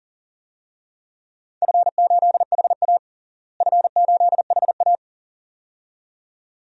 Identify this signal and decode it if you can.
Morse code, heard as a keyed tone, about 40 wpm, 700 Hz. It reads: F8HA F8HA
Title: DXCC-100 Call Signs - 4 Characters - Repeated 2x - 40wpm